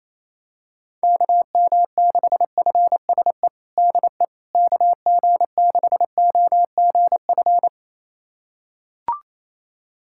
Morse code, heard as a keyed tone, about 28 wpm, 700 Hz. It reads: KM6FSE DE KG6OGF
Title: USA Callsign DE Callsign - 6 Characters - 28wpm